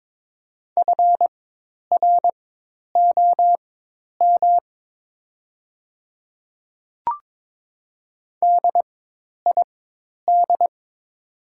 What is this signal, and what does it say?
Morse code, heard as a keyed tone, about 21 wpm, spaced as for 10 wpm, 700 Hz. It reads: FROM DID
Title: 4x Character Spacing - Top 100 Words - 22wpm